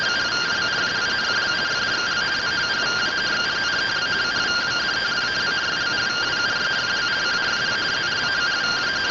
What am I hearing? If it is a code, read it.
Two-tone teleprinter signal, about 170 Hz shift.